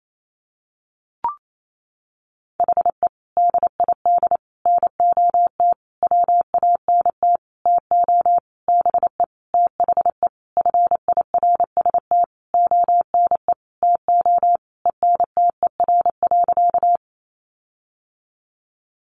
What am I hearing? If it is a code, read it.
Morse code, heard as a keyed tone, about 28 wpm, 700 Hz. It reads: HE DID NOT WANT TO BE THE FIRST ONE TO ENTER.
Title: Sentences from Top 1300 Words 28wpm